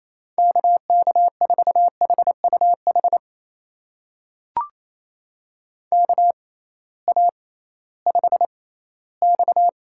Morse code, heard as a keyed tone, about 28 wpm, spaced as for 16 wpm, 700 Hz. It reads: KK4HUH KA5X
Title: USA Call Signs - 6x Character Spacing - 28wpm